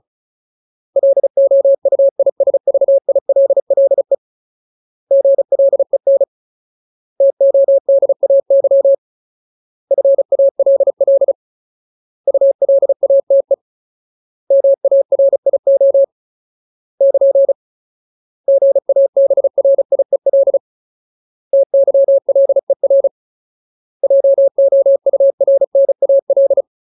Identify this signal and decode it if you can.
Morse code, heard as a keyed tone, about 34 wpm, 550 Hz. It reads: LOUISVILLE GLEN TODAY FALL ULATE MARIO <KN> GABRIEL TYLER JOURNAL
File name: Ultimate Word List / 35WPM Ultimate